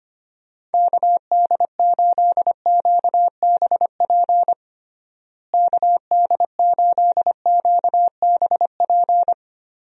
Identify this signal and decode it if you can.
Morse code, heard as a keyed tone, about 25 wpm, 700 Hz. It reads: KD8QBP KD8QBP
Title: USA Call Signs - 6 Characters - Repeated 3x - 25wpm